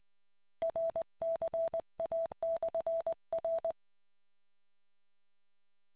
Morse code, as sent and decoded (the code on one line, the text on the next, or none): .-. -.-. .- -..-. .-.
RCA/R